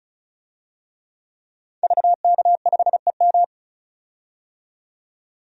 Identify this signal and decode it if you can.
Morse code, heard as a keyed tone, about 36 wpm, 700 Hz. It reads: VK5EM
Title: DXCC-100 Call Signs - 5 Characters - 35wpm